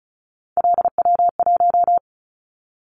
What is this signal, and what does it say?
Morse code, heard as a keyed tone, about 34 wpm, 700 Hz. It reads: LW1
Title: Callsign Prefixes for Top 50 DXCC - 35wpm